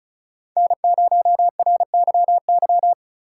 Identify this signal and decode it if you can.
Morse code, heard as a keyed tone, about 36 wpm, 700 Hz. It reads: N0RYY
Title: USA Call Signs - 5 Characters - 3x Character Spacing - 35wpm